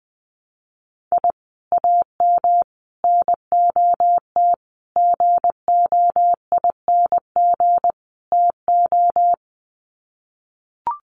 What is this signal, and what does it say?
Morse code, heard as a keyed tone, about 20 wpm, 700 Hz. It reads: I AM NOT GOING TO
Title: Sets of 5 Words Encore 20wpm